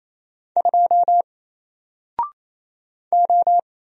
Morse code, heard as a keyed tone, about 27 wpm, 700 Hz. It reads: 2 O
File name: Single-Letter-Number-28wpm